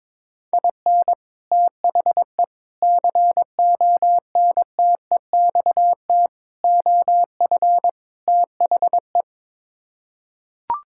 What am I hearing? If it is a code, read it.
Morse code, heard as a keyed tone, about 22 wpm, 700 Hz. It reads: IN THE CONTEXT OF THE
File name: practice / 22wpm / Build A Phrase - 5-Word Phrases - 22wpm